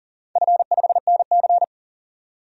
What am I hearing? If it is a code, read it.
Morse code, heard as a keyed tone, about 40 wpm, 700 Hz. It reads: F5NC
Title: DXCC-100 Call Signs - 4 Characters - 40wpm